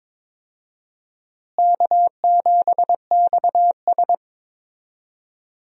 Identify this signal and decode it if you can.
Morse code, heard as a keyed tone, about 22 wpm, 700 Hz. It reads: K7XS
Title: USA Call Signs - 4 Characters - Repeated 2x - 22wpm